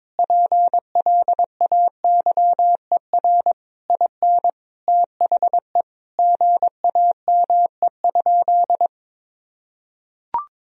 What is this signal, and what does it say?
Morse code, heard as a keyed tone, about 22 wpm, 700 Hz. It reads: PLAYER IN THE GAME?
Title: Sentences from Top 1000 Words - Review Part 4 - 22wpm